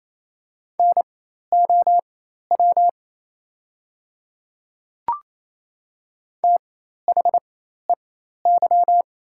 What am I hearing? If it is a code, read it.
Morse code, heard as a keyed tone, about 28 wpm, spaced as for 13 wpm, 690 Hz. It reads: NOW THEY
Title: 4x Character Spacing - Top 100 Words - 28wpm